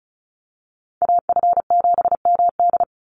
Morse code, heard as a keyed tone, about 34 wpm, 700 Hz. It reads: AF7MD
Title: NA SOTA Chaser Callsigns 35wpm